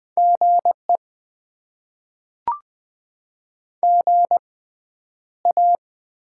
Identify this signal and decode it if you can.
Morse code, heard as a keyed tone, about 20 wpm, spaced as for 10 wpm, 700 Hz. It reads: GE GA
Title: Top 20 QSO Elements - 6x Character Spacing - 20wpm